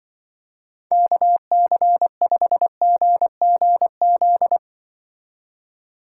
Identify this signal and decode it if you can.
Morse code, heard as a keyed tone, about 24 wpm, 700 Hz. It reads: KC5GGZ
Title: Speed Racing - USA 2x3 Call Signs 20wpm